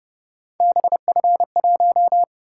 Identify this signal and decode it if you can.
Morse code, heard as a keyed tone, about 31 wpm, 700 Hz. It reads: BF1